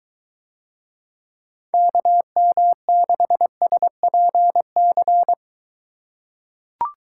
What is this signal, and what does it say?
Morse code, heard as a keyed tone, about 23 wpm, 700 Hz. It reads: KM6SPC